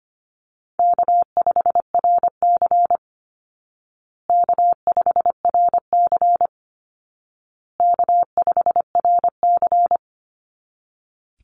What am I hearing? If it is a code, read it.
Morse code, heard as a keyed tone, about 25 wpm, 700 Hz. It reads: K5RC K5RC K5RC
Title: Call Signs Repeated 2x - 25wpm